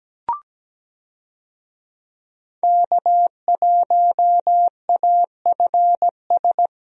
Morse code, heard as a keyed tone, about 17 wpm, 700 Hz. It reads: K1AFS
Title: USA Call Signs - 5 Characters - 17wpm